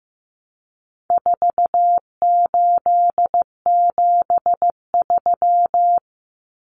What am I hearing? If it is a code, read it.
Morse code, heard as a keyed tone, about 15 wpm, 700 Hz. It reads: 4873
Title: Common Cut Numbers - 4-Digit 15wpm